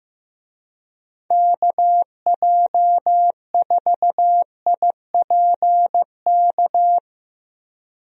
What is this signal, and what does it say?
Morse code, heard as a keyed tone, about 15 wpm, 700 Hz. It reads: KJ4IPK
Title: USA Call Signs - 6 Characters - 15wpm